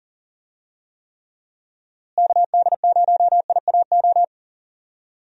Morse code, heard as a keyed tone, about 40 wpm, 700 Hz. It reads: KD0IAO
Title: USA Call Signs - 6 Characters - Repeated 3x - 40wpm